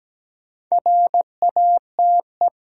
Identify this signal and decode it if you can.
Morse code, heard as a keyed tone, about 17 wpm, 700 Hz. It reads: RATE
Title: Sentences from Top 700 Words - New Word Drill - 17wpm